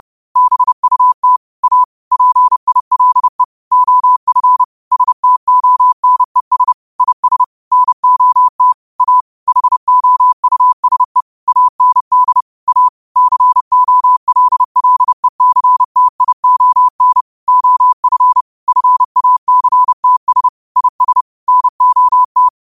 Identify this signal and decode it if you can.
Morse code, heard as a keyed tone, about 30 wpm, 1000 Hz. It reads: BUT A PILE OF STONES IS NOT A HOUSE AND A COLLECTION OF FACTS IS NOT
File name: qotdfiles / 30wpm